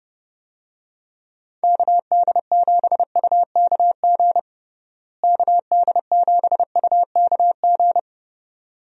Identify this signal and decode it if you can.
Morse code, heard as a keyed tone, about 30 wpm, 700 Hz. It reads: KD7UKG KD7UKG
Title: USA Call Signs - 6 Characters - Repeated 2x - 30wpm